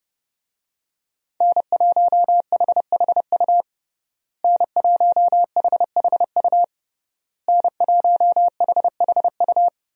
Morse code, heard as a keyed tone, about 29 wpm, 700 Hz. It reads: N1HHU N1HHU N1HHU
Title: USA Call Signs - 5 Characters - Repeated 3x - 30wpm